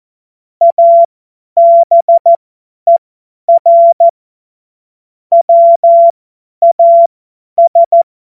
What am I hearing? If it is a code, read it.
Morse code, heard as a keyed tone, about 14 wpm, spaced as for 10 wpm, 680 Hz. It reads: ABER WAS
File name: Morsekurs / Witze / Witz 23. August 2022 50 F70